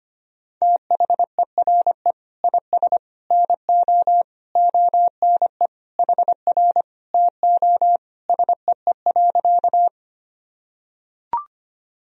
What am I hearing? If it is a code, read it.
Morse code, heard as a keyed tone, about 25 wpm, 700 Hz. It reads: THERE IS NO ONE HR TO SEE.
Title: Sentences from Top 200 Words and Abbreviations 25wpm